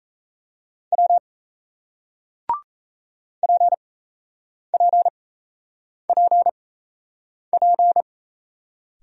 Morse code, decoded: W PPPP

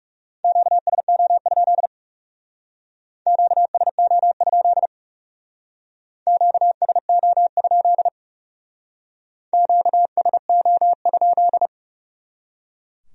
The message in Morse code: QSO? QSO? QSO? QSO?

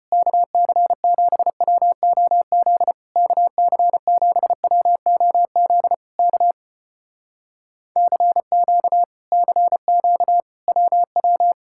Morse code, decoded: KC7WOZ KC7WOZ K CQ CQ WW

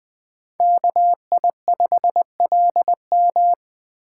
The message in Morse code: KI5LM

